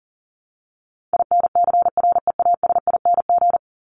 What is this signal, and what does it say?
Morse code, heard as a keyed tone, about 40 wpm, 700 Hz. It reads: INCREASING
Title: ING Suffix - Top 100 Words with Base Words 40wpm